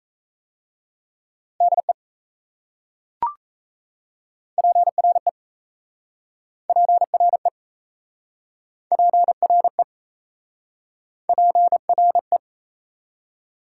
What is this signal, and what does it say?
Morse code, heard as a keyed tone, about 38 wpm, 700 Hz. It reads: DE PRE PRE PRE PRE